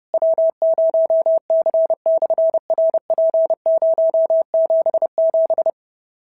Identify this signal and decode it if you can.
Morse code, heard as a keyed tone, about 30 wpm, 650 Hz. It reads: W0C/RP077
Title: SOTA QSO Practice 30wpm